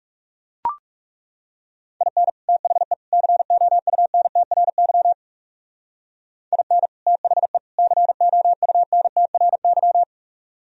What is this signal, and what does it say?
Morse code, heard as a keyed tone, about 36 wpm, 700 Hz. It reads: IN THE COUNTRY IN THE COUNTRY
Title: Speed-Racing - Sets of 3 Words 30wpm